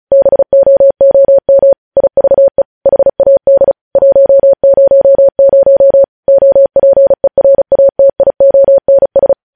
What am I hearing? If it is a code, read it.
Morse code, heard as a keyed tone, about 36 wpm, 550 Hz. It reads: BOOM IVE HAD 100 OPERATIONS
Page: News Headlines in Morse Code at 35 WPM